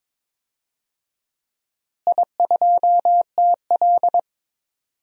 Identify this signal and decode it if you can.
Morse code, heard as a keyed tone, about 22 wpm, 700 Hz. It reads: I2TL
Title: DXCC-100 Call Signs - 4 Characters - Repeated 3x - 22wpm